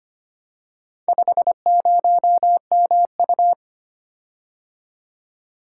Morse code, heard as a keyed tone, about 25 wpm, 700 Hz. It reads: 50MU
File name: practice / 25wpm / Four Letter-Number Combinations - 25wpm